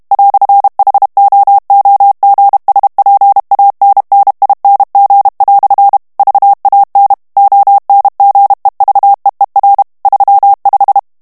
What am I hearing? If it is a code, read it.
Morse code, heard as a keyed tone, about 32 wpm, 800 Hz. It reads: "HOOGSPANNING" VAN ONGEVEER 35
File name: HOLLANDAIS texte clair 2 vitesse 30 mots minute